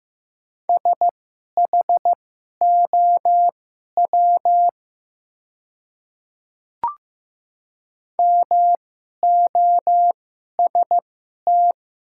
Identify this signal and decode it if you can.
Morse code, heard as a keyed tone, about 15 wpm, spaced as for 11 wpm, 700 Hz. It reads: SHOW MOST
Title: 2x Character Spacing - Top 300 Words - 15wpm